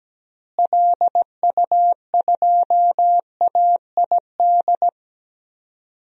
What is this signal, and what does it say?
Morse code, heard as a keyed tone, about 17 wpm, 700 Hz. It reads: LU2AID
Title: DXCC-100 Call Signs - 6 Characters - Repeated 3x - 17wpm